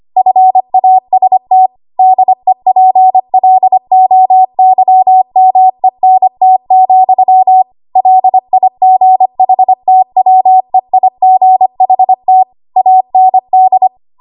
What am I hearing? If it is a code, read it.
Morse code, heard as a keyed tone, about 25 wpm, 750 Hz. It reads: FAST DEPLOYMENT, LIGHTWEIGHT AND
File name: Archive / 25 WPM / 25WPM